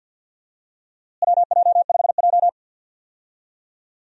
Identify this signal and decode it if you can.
Morse code, heard as a keyed tone, about 46 wpm, 700 Hz. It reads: WJ5J
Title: USA Call Signs - 4 Characters - Repeated 2x - 50wpm